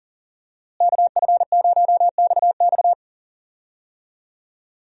Morse code, heard as a keyed tone, about 40 wpm, 700 Hz. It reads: KF0XX
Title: USA Call Signs - 5 Characters - Repeated 2x - 40wpm